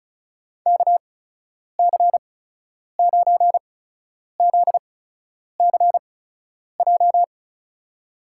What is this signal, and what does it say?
Morse code, heard as a keyed tone, about 36 wpm, spaced as for 10 wpm, 700 Hz. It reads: KC9ZCJ
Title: USA Call Signs - 6 Characters - 8x Character Spacing - 35wpm